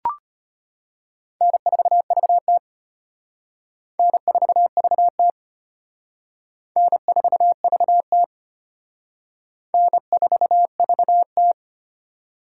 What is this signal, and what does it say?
Morse code, heard as a keyed tone, about 31 wpm, 700 Hz. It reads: N4VT N4VT N4VT N4VT
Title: Speed Racing - USA 1x2 Call Signs 25wpm